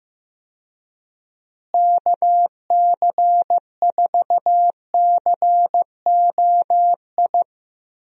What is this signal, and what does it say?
Morse code, heard as a keyed tone, about 16 wpm, 700 Hz. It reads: KC4COI
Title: USA Call Signs - 6 Characters - Repeated 2x - 15wpm